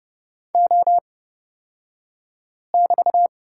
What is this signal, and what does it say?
Morse code, heard as a keyed tone, about 30 wpm, spaced as for 5 wpm, 700 Hz. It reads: O<BT>
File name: Lesson 36d - BT - Two Characters 30wpm